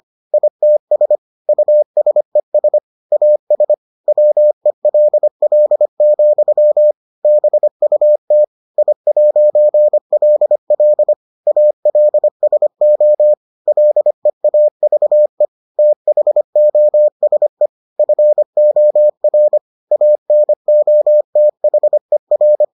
Morse code, heard as a keyed tone, about 25 wpm, 600 Hz. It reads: ITS USES AS WELL, BUT I'LL ALSO LEAVE THOSE FOR ANOTHER